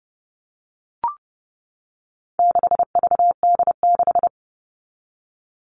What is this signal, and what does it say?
Morse code, heard as a keyed tone, about 30 wpm, 700 Hz. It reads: 6VD6
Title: All Cut Numbers - 4-Digit 30wpm